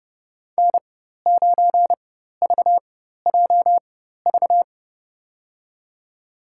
Morse code, heard as a keyed tone, about 32 wpm, spaced as for 14 wpm, 700 Hz. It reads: N9VJV
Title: USA Call Signs - 5 Characters - 4x Character Spacing - 30wpm